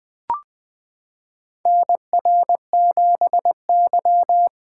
Morse code, heard as a keyed tone, about 20 wpm, 700 Hz. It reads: NR7Y
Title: 2023 SOTA Chaser Call Signs - 20wpm